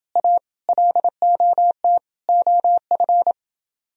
Morse code, heard as a keyed tone, about 29 wpm, 700 Hz. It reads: A LOT OF